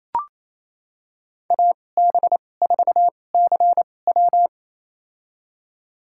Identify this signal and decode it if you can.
Morse code, heard as a keyed tone, about 28 wpm, spaced as for 20 wpm, 700 Hz. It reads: AB4CW
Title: USA Call Signs - 2x Character Spacing - 28wpm